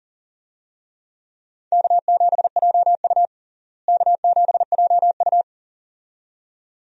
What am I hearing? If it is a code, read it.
Morse code, heard as a keyed tone, about 40 wpm, 700 Hz. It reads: K7JU K7JU
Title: USA Call Signs - 4 Characters - Repeated 2x - 40wpm